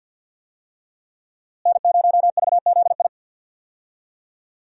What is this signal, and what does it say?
Morse code, heard as a keyed tone, about 49 wpm, 700 Hz. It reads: N0VZI
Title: USA Call Signs - 5 Characters - Repeated 2x - 50wpm